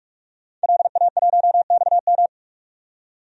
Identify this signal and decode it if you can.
Morse code, heard as a keyed tone, about 41 wpm, 700 Hz. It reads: LA1XM
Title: DXCC-100 Call Signs - 5 Characters - 45wpm